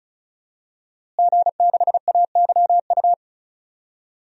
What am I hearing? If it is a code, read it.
Morse code, heard as a keyed tone, about 36 wpm, 700 Hz. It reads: G6AYU
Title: DXCC-100 Call Signs - 5 Characters - Repeated 3x - 35wpm